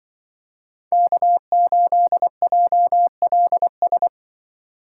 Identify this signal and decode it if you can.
Morse code, heard as a keyed tone, about 24 wpm, 700 Hz. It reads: K8JLS
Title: Speed Racing - USA Call Signs 20wpm